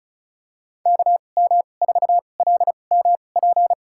Morse code, heard as a keyed tone, about 34 wpm, spaced as for 25 wpm, 700 Hz. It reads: KM4LMP
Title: USA Call Signs - 6 Characters - 2x Character Spacing - 35wpm